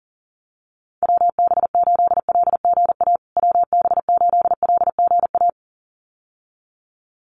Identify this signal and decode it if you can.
Morse code, heard as a keyed tone, about 40 wpm, 700 Hz. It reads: WB8LGA WB8LGA
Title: USA Call Signs Repeated 2x 40wpm